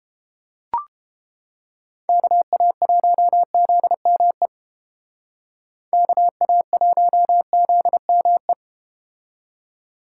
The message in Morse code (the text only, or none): KA1ZME KA1ZME